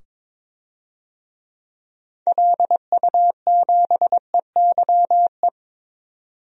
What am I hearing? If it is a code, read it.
Morse code, heard as a keyed tone, about 22 wpm, 700 Hz. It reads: LU7EYE
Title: DXCC-100 Call Signs - 6 Characters - 22wpm